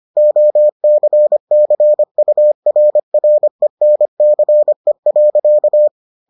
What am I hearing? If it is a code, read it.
Morse code, heard as a keyed tone, about 25 wpm, 600 Hz. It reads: OCCURRENCE.